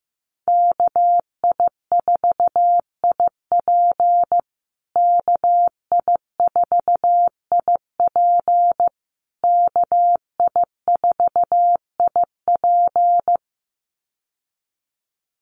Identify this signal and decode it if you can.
Morse code, heard as a keyed tone, about 15 wpm, 700 Hz. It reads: KI4IP KI4IP KI4IP
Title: USA Call Signs Repeated 3x 15wpm